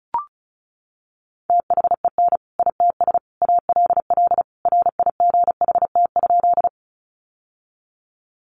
Morse code, heard as a keyed tone, about 34 wpm, 700 Hz. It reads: THEN ITS ALL RIGHT?
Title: Sentences from Top 600 Words 35wpm